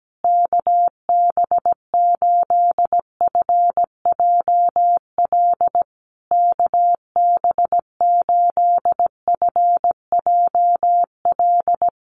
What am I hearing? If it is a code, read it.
Morse code, heard as a keyed tone, about 17 wpm, 700 Hz. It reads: KB8FJL KB8FJL
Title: USA Call Signs Repeated 3x 17wpm